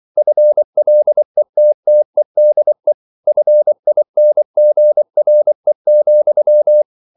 Morse code, heard as a keyed tone, about 24 wpm, 600 Hz. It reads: FLETTEDE FINGRE,
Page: Ferie 24 wpm